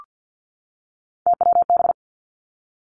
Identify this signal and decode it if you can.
Morse code, heard as a keyed tone, about 50 wpm, 700 Hz. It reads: TWB
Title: Three Challenging Letter-Number Combinations 50wpm